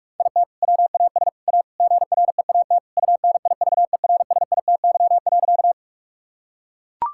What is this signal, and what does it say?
Morse code, heard as a keyed tone, about 41 wpm, 690 Hz. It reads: IT WAS A GREAT UNIVERSITY.
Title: Sentences from Top 1000 Words - Review Part 5 - 45wpm